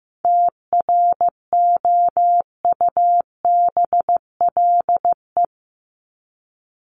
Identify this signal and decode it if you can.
Morse code, heard as a keyed tone, about 15 wpm, 700 Hz. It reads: TROUBLE
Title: Top 1001-1250 Words 15wpm